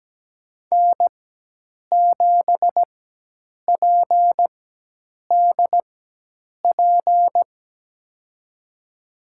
Morse code, heard as a keyed tone, about 17 wpm, spaced as for 8 wpm, 700 Hz. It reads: N7PDP
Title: USA Call Signs - 5 Characters - 4x Character Spacing - 17wpm